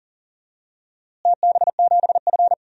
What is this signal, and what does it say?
Morse code, heard as a keyed tone, about 40 wpm, 700 Hz. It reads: TB7F
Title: DXCC-100 Call Signs - 4 Characters - Repeated 2x - 40wpm